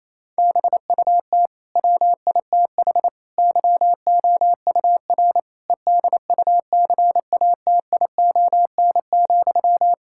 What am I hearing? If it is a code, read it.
Morse code, heard as a keyed tone, about 28 wpm, 700 Hz. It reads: BUT WITH YOUR EDUCATION,
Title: Sentences from Top 1000 Words - Review Part 6 - 28wpm